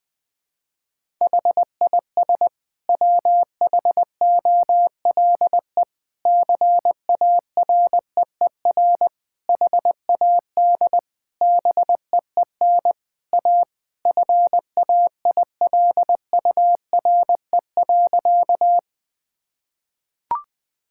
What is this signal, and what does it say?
Morse code, heard as a keyed tone, about 20 wpm, 700 Hz. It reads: HIS WHOLE CAREER HAD BEEN A FAILURE.
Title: Sentences from Top 1000 Words - Review Part 6 - 20wpm